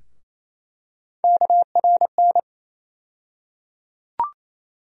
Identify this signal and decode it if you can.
Morse code, heard as a keyed tone, about 28 wpm, 700 Hz. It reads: KRN
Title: USA Call Sign Postfixes - 3 Letters 28wpm